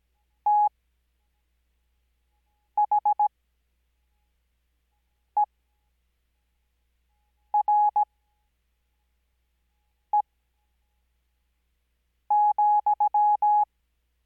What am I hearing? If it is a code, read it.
Morse code, heard as a keyed tone, about 17 wpm, spaced as for 4 wpm, 840 Hz. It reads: THERE,
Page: Farnsworth Spacing: 15WPM Character speed Spaced at 5WPM.